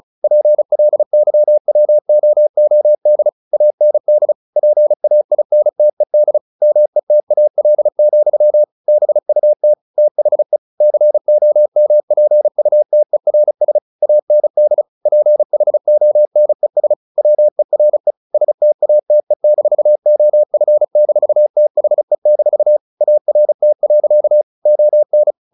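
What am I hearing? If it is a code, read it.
Morse code, heard as a keyed tone, about 35 wpm, 600 Hz. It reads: PLYWOOD AND PAINTED METAL, BUT THE COMPUTERS AND PHONES WERE STATE-OF-THE-ART. ON
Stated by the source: Morse practice at 35 wpm from CW med Gnister, sending Ken Follett's novel Never in English.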